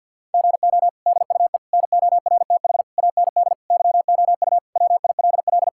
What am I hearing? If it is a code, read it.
Morse code, heard as a keyed tone, about 52 wpm, 700 Hz. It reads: GO DUE NORTH AND YOU WILL
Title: Sentences from Top 1000 Words - Review Part 6 - 50wpm